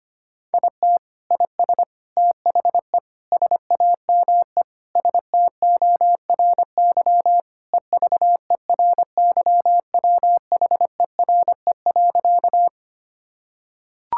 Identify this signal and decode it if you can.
Morse code, heard as a keyed tone, about 25 wpm, 700 Hz. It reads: IT IS THE SAME STORY EVERYWHERE.
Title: Sentences from Top 1600 Words 25wpm